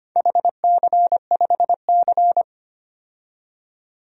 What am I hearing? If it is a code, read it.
Morse code, heard as a keyed tone, about 25 wpm, 700 Hz. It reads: HC5C
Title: DXCC-100 Call Signs - 4 Characters - 25wpm